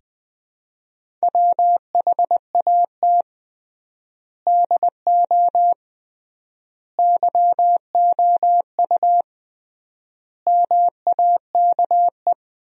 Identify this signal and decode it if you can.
Morse code, heard as a keyed tone, about 20 wpm, 700 Hz. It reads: WHAT DO YOU MAKE
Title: 3x Word-Spacing - Sentences from Top 100 Words - 20wpm